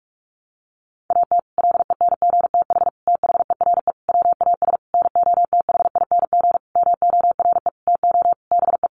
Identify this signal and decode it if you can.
Morse code, heard as a keyed tone, about 44 wpm, 700 Hz. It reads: AT LENGTH THERE WAS NOTHING MORE TO BE
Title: Sentences from Top 1000 Words 45wpm